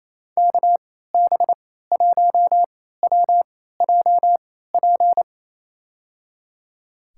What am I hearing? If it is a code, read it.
Morse code, heard as a keyed tone, about 28 wpm, spaced as for 16 wpm, 700 Hz. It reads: KB1WJP
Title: USA Call Signs - 3x Character Spacing - 28wpm